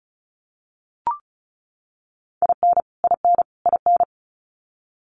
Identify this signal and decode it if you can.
Morse code, heard as a keyed tone, about 34 wpm, 700 Hz. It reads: IN IN IN
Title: Top 200 Words Repeated 35wpm